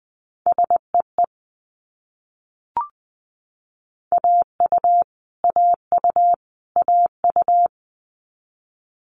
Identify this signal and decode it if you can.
Morse code, heard as a keyed tone, about 20 wpm, 700 Hz. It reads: SEE AU AU AU
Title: Multi-letter Phonemes Repeated 3x 20wpm